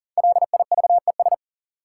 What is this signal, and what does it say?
Morse code, heard as a keyed tone, about 39 wpm, 700 Hz. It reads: LIVES
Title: Sets of 5 Words Encore - New Word Drill - 40wpm